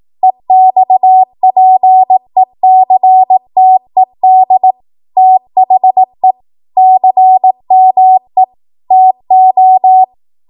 Morse code, heard as a keyed tone, about 18 wpm, 750 Hz. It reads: EXPECTED THE CME TO
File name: Archive / 18 WPM / 18WPM